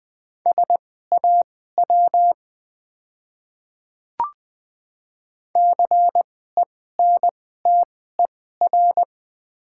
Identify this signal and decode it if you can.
Morse code, heard as a keyed tone, about 20 wpm, spaced as for 14 wpm, 690 Hz. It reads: SAW CENTER
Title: 2x Character Spacing - Top 500 Words - 20wpm